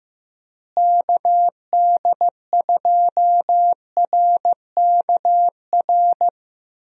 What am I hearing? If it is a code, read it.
Morse code, heard as a keyed tone, about 15 wpm, 700 Hz. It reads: KD2RKR
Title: USA Call Signs - 6 Characters - Repeated 2x - 15wpm